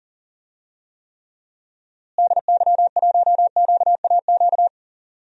Morse code, heard as a keyed tone, about 40 wpm, 700 Hz. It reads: DY1QAQ
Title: DXCC-100 Call Signs - 6 Characters - Repeated 2x - 40wpm